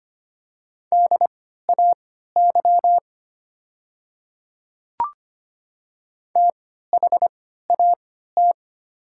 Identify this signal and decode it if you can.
Morse code, heard as a keyed tone, about 25 wpm, spaced as for 14 wpm, 700 Hz. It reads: DAY THAT
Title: 3x Character Spacing - Top 100 Words - 25wpm